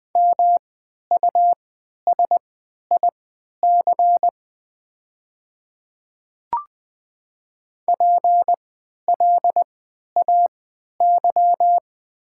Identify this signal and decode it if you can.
Morse code, heard as a keyed tone, about 21 wpm, spaced as for 12 wpm, 700 Hz. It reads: MUSIC PLAY
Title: 3x Character Spacing - Top 300 Words - 20wpm